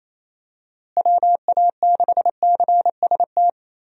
Morse code, heard as a keyed tone, about 28 wpm, 700 Hz. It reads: WA6CST
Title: USA Call Signs - 6 Characters - Repeated 3x - 28wpm